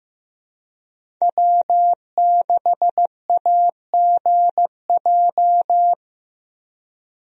Morse code, heard as a keyed tone, about 15 wpm, 700 Hz. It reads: W6AGJ
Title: USA Call Signs - 5 Characters - Repeated 3x - 15wpm